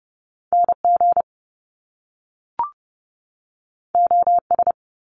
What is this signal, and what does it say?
Morse code, heard as a keyed tone, about 30 wpm, 700 Hz. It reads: NG OS
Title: Top Two Letter Combinations in Words 30wpm